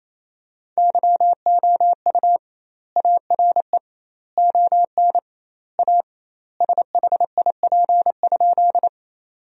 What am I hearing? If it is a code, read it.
Morse code, heard as a keyed tone, about 28 wpm, 700 Hz. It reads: YOU ARE ON A SHIP?
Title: 2x Word-Spacing - Sentences from Top 400 Words - 28wpm